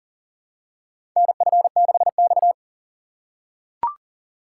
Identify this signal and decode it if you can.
Morse code, heard as a keyed tone, about 40 wpm, 700 Hz. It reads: NF6X
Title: USA Call Signs - 4 Characters - 2x Character Spacing - 40wpm